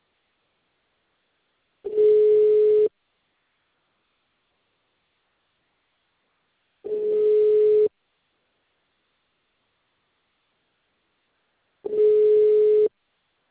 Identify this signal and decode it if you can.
Ringback tone (Europe, 425 Hz).